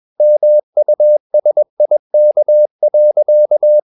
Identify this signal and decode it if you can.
Morse code, heard as a keyed tone, about 21 wpm, 600 Hz. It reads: MUSIK.
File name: Ferie QRS 21wpm